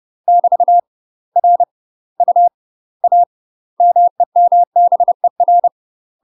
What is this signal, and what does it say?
Morse code, heard as a keyed tone, about 30 wpm, 700 Hz. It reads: <BT> R U A MEMBER
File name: UK MASTER QSOs TRAINING / 30WPM / UK MASTER QSOs 0039